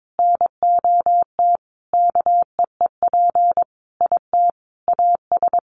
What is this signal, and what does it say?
Morse code, heard as a keyed tone, about 22 wpm, 700 Hz. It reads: NOT KEEP IT AS
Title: Sentences from Top 1200 Words 22wpm